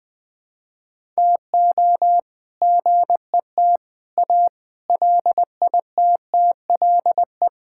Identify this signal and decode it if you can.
Morse code, heard as a keyed tone, about 20 wpm, 700 Hz. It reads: TO GET A LITTLE
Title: All-Age Appropriate Jokes - 20wpm